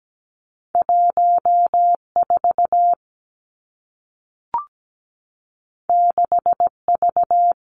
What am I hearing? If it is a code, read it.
Morse code, heard as a keyed tone, about 17 wpm, 700 Hz. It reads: 1 4 6 V
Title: Two Challenging Letter-Number Combinations 17wpm